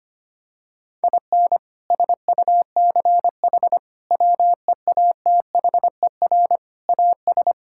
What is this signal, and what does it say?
Morse code, heard as a keyed tone, about 25 wpm, 700 Hz. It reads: IN SUCH WEATHER AS
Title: Sentences from Top 1000 Words - Review Part 5 - 25wpm